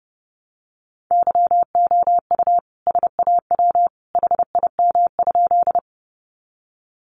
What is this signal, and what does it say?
Morse code, heard as a keyed tone, about 30 wpm, 700 Hz. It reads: YOU SAW HIM?
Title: Sentences from Top 300 Words 30wpm